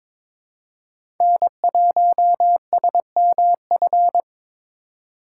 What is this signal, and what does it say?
Morse code, heard as a keyed tone, about 22 wpm, 700 Hz. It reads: N1SMF
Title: USA Call Signs - 5 Characters - 4x Character Spacing - 22wpm